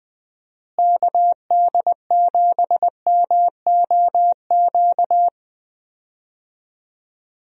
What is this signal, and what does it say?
Morse code, heard as a keyed tone, about 21 wpm, 700 Hz. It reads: KD7MOQ
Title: USA Call Signs - 6 Characters - Repeated 3x - 20wpm